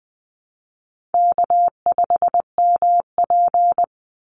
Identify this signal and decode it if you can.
Morse code, heard as a keyed tone, about 20 wpm, 700 Hz. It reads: K5MP
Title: Easy Call Signs Repeated 3x - 20wpm